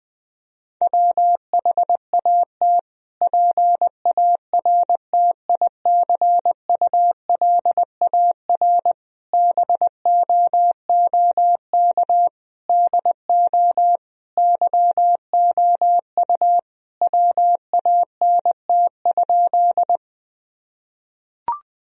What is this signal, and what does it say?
Morse code, heard as a keyed tone, about 20 wpm, 700 Hz. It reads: WHAT PARTICULAR BOOK DO YOU WANT?
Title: Sentences from Top 1000 Words - Review Part 4 - 20wpm